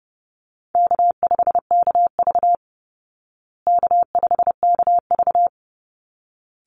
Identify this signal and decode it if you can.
Morse code, heard as a keyed tone, about 30 wpm, 700 Hz. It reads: K5KV K5KV
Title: Call Signs Repeated 3x - 30wpm